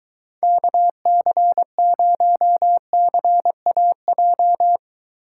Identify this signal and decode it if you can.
Morse code, heard as a keyed tone, about 23 wpm, 700 Hz. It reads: KC0CAJ